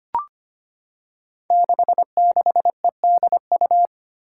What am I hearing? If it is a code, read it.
Morse code, heard as a keyed tone, about 25 wpm, 700 Hz. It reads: 66EDU
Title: Five Challenging Letter-Number Combinations 25wpm